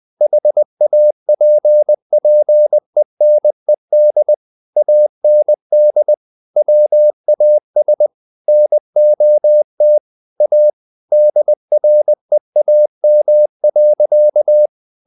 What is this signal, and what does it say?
Morse code, heard as a keyed tone, about 20 wpm, 600 Hz. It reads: HAPPENED AND WAS NOT A DREAM.
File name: the old man and the sea part 2 20wpm